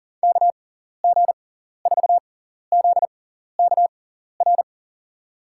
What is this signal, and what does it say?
Morse code, heard as a keyed tone, about 40 wpm, spaced as for 14 wpm, 700 Hz. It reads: KG4ZKR
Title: USA Call Signs - 6x Character Spacing - 40wpm